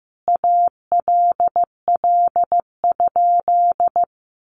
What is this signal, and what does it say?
Morse code, heard as a keyed tone, about 15 wpm, 700 Hz. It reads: ALL?
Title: Sentences from Top 200 Words 15wpm